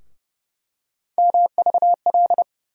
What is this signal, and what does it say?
Morse code, heard as a keyed tone, about 30 wpm, 700 Hz. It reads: MVL
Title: USA Call Sign Postfixes - 3 Letters 30wpm